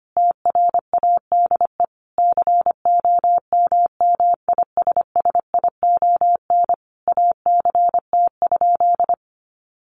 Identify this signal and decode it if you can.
Morse code, heard as a keyed tone, about 25 wpm, 700 Hz. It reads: TRADE COMMISSION ACT?
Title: Sentences from Top 1400 Words 25wpm